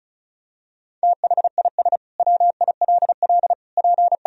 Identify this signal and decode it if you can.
Morse code, heard as a keyed tone, about 36 wpm, 700 Hz. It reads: THIS WILL P
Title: Sentences from Top 400 Words and Abbreviations 35wpm